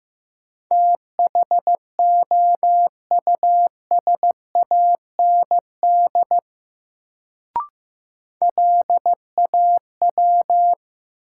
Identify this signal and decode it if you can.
Morse code, heard as a keyed tone, about 15 wpm, 700 Hz. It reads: THOUSAND LAW
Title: Sentences from Top 600 Words - New Word Drill - 15wpm